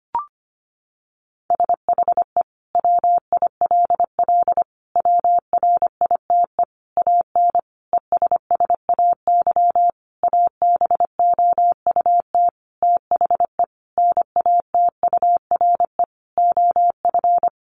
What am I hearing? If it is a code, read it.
Morse code, heard as a keyed tone, about 25 wpm, 700 Hz. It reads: SHE WILL WRITE AN ESSAY ABOUT THE NATURE OF
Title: Sentences from Top 1500 Words 25wpm